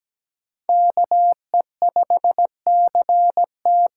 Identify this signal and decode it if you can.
Morse code, heard as a keyed tone, about 17 wpm, 700 Hz. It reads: KE5CT
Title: USA Call Signs - 6 Characters - 8x Character Spacing - 17wpm